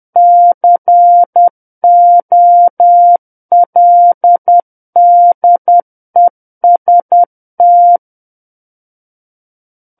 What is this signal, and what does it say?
Morse code, heard as a keyed tone, about 10 wpm, 700 Hz. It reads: COLDEST
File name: Book One / 01-Chapter 1 - 10 WPM